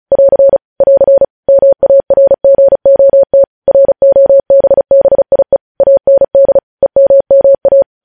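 Morse code, heard as a keyed tone, about 36 wpm, 550 Hz. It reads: <AR> <AR> MARGOT ROBBIE AND EMMA
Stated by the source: Morse code transmission of news headlines, preceded by vvv.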